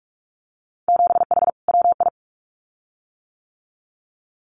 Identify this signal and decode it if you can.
Morse code, heard as a keyed tone, about 46 wpm, 700 Hz. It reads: 7H WI
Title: Winter Field Day Exchanges Encore 45wpm